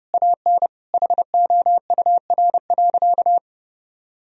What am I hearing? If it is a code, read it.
Morse code, heard as a keyed tone, about 32 wpm, 700 Hz. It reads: AN HOUR.
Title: Sentences from Top 1000 Words - Review Part 4 - 30wpm